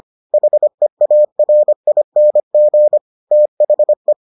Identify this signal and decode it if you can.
Morse code, heard as a keyed tone, about 25 wpm, 600 Hz. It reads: HEARING THE